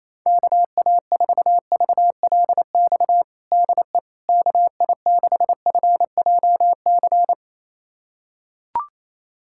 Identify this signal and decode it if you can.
Morse code, heard as a keyed tone, about 27 wpm, 700 Hz. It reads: KA4VLX DE KI6FJC
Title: USA Callsign DE Callsign - 6 Characters - 28wpm